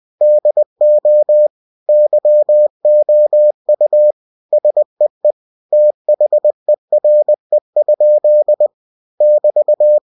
Morse code, heard as a keyed tone, about 20 wpm, 600 Hz. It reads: DO YOU SEE THERE? <BT>
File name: winnie-20wpm